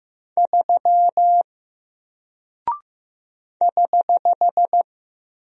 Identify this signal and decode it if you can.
Morse code, heard as a keyed tone, about 15 wpm, 700 Hz. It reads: 3 <HH>
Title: Single Character (Letter:Number:Punctuation:Prosign) Alt 15wpm